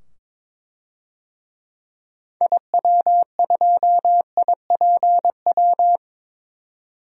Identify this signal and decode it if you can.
Morse code, heard as a keyed tone, about 22 wpm, 700 Hz. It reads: IW2IPW
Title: DXCC-100 Call Signs - 6 Characters - 22wpm